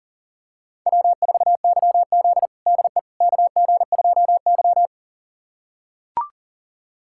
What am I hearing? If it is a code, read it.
Morse code, heard as a keyed tone, about 40 wpm, 700 Hz. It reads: W4YZ DE KG2Y
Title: USA Callsign DE Callsign - 4 Characters - 40wpm